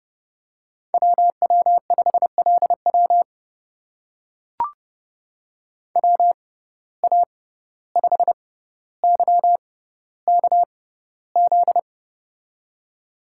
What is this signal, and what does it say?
Morse code, heard as a keyed tone, about 30 wpm, spaced as for 15 wpm, 700 Hz. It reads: WW5LW WA5YKZ
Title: USA Call Signs - 6x Character Spacing - 30wpm